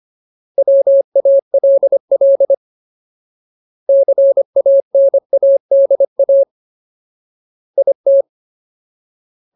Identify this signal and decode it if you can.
Morse code, heard as a keyed tone, about 25 wpm, 550 Hz. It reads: WALL CANADA IT
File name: Ultimate Word List / 25WPM Ultimate